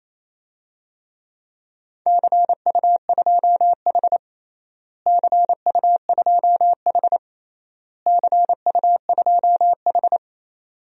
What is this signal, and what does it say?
Morse code, heard as a keyed tone, about 28 wpm, 700 Hz. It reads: CU2H CU2H CU2H
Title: DXCC-100 Call Signs - 4 Characters - Repeated 3x - 28wpm